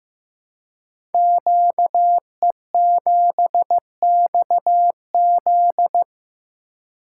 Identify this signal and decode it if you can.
Morse code, heard as a keyed tone, about 15 wpm, 700 Hz. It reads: QE7XZ
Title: Code Groups - 5 Letter:Numbers - 15wpm